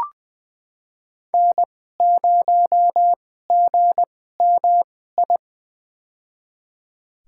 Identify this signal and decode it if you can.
Morse code, heard as a keyed tone, about 20 wpm, spaced as for 14 wpm, 700 Hz. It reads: N0GMI
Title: USA Call Signs - 2x Character Spacing - 20wpm